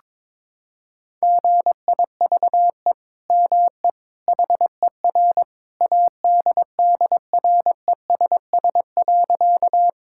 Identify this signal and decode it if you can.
Morse code, heard as a keyed tone, about 22 wpm, 700 Hz. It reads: GIVE ME HER ADDRESS.
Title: Sentences from Top 1000 Words - Review Part 6 - 22wpm